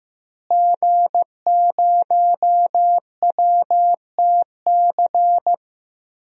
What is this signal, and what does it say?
Morse code, heard as a keyed tone, about 15 wpm, 700 Hz. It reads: G0WTC
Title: DXCC-100 Call Signs - 5 Characters - Repeated 2x - 15wpm